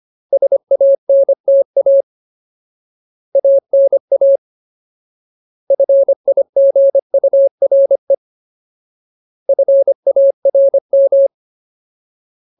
Morse code, heard as a keyed tone, about 25 wpm, 550 Hz. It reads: SANTA ANA FIGURE FARM
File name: Ultimate Word List / 25WPM Ultimate